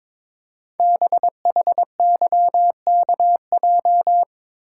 Morse code, heard as a keyed tone, about 22 wpm, 700 Hz. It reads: BHYKJ
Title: Code Groups - 5 Letters - 22wpm